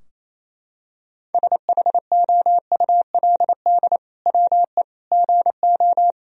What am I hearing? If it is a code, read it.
Morse code, heard as a keyed tone, about 28 wpm, 700 Hz. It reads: SHOULD WE GO
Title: Sentences from Top 1000 Words - Review Part 3 - 28wpm